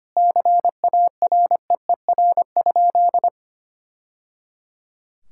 Morse code, heard as a keyed tone, about 25 wpm, 700 Hz. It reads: CAREER?
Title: Sentences from Top 1000 Words - Review Part 6 - 25wpm